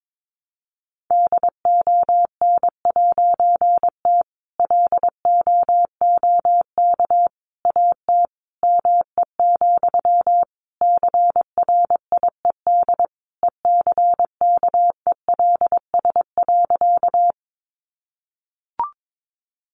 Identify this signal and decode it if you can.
Morse code, heard as a keyed tone, about 22 wpm, 700 Hz. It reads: DON'T LOOK AT ME, CRIED ECKELS.
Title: A Sound Of Thunder 22wpm